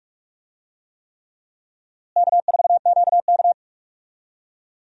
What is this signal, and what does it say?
Morse code, heard as a keyed tone, about 46 wpm, 700 Hz. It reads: K4QK
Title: USA Call Signs - 4 Characters - Repeated 3x - 45wpm